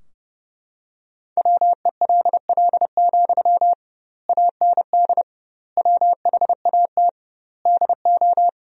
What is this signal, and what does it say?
Morse code, heard as a keyed tone, about 30 wpm, 700 Hz. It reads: WELL, AND WHAT DO
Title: 2x Word-Spacing - Sentences from Top 200 Words - 30wpm